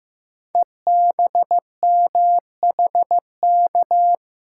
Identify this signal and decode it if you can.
Morse code, heard as a keyed tone, about 15 wpm, 700 Hz. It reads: EBMHK
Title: Code Groups - 5 Letters - 15wpm